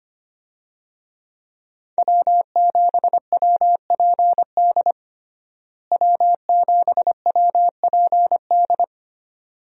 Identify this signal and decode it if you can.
Morse code, heard as a keyed tone, about 25 wpm, 700 Hz. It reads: W7WPD W7WPD
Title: USA Call Signs - 5 Characters - Repeated 2x - 25wpm